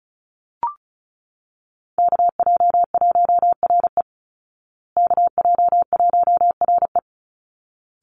Morse code, heard as a keyed tone, about 34 wpm, 700 Hz. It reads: KJ1RE KJ1RE
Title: Call Signs Repeated 3x - 35wpm